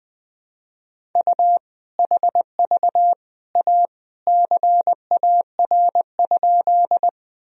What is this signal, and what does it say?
Morse code, heard as a keyed tone, about 20 wpm, 700 Hz. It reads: U HV A CAR?
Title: Sentences from Top 300 Words and Abbreviations 20wpm